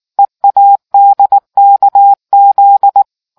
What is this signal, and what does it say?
Morse code, heard as a keyed tone, about 19 wpm, 800 Hz. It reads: EADKZ